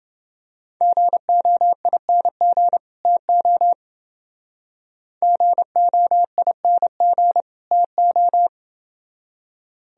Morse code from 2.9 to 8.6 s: TO GOING TO

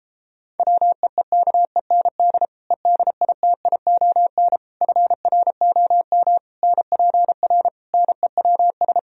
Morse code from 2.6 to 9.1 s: EDITION FROM NPR NEWS

